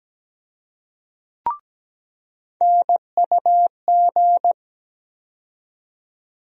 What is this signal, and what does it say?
Morse code, heard as a keyed tone, about 17 wpm, 700 Hz. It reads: NUG
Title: USA Call Sign Postfixes - 3 Letters 17wpm